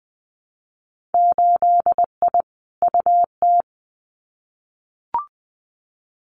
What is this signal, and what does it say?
Morse code, heard as a keyed tone, about 20 wpm, 700 Hz. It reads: 8I UT
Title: Winter Field Day Exchanges Encore 20wpm